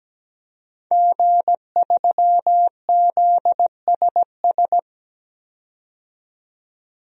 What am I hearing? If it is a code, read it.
Morse code, heard as a keyed tone, about 17 wpm, 700 Hz. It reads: G3ZSS
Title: DXCC-100 Call Signs - 5 Characters - 17wpm